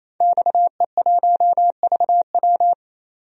Morse code, heard as a keyed tone, about 28 wpm, 700 Hz. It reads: XE1VW
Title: DXCC-100 Call Signs - 5 Characters - 28wpm